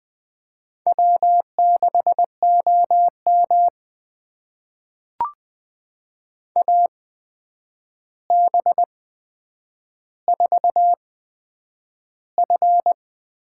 Morse code, heard as a keyed tone, about 20 wpm, spaced as for 9 wpm, 700 Hz. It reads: W6OM AB4F
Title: USA Call Signs - 4 Characters - 8x Character Spacing - 20wpm